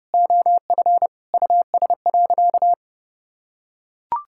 Morse code, heard as a keyed tone, about 32 wpm, 700 Hz. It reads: OF US.
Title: Sentences from Top 1000 Words - Review Part 6 - 30wpm